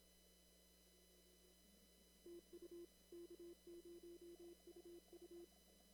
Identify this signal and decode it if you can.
Morse code, heard as a keyed tone, about 26 wpm, 340 Hz. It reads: TU K0UU
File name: CW / kt5j wpx cw-op-k5pi